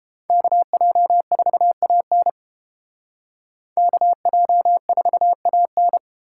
Morse code, heard as a keyed tone, about 30 wpm, 700 Hz. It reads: KJ4AN KJ4AN